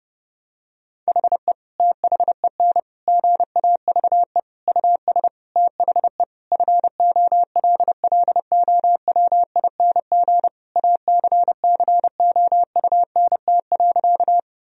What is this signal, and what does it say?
Morse code, heard as a keyed tone, about 30 wpm, 700 Hz. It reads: HE THEN GAVE US THE FOLLOWING ACCOUNT.
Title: Sentences from Top 1000 Words - Review Part 4 - 30wpm